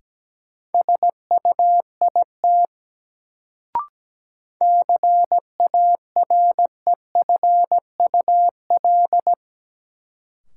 Morse code, heard as a keyed tone, about 17 wpm, 700 Hz. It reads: SUIT CAREFUL
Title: Sentences from Top 1600 Words - New Word Drill - 17wpm